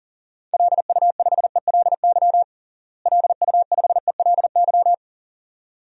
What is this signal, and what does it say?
Morse code, heard as a keyed tone, about 37 wpm, 700 Hz. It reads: LU5ELY LU5ELY
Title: DXCC-100 Call Signs - 6 Characters - Repeated 2x - 40wpm